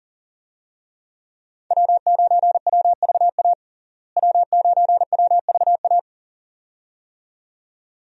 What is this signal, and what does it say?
Morse code, heard as a keyed tone, about 40 wpm, 700 Hz. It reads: W9WVA W9WVA
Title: USA Call Signs - 5 Characters - Repeated 2x - 40wpm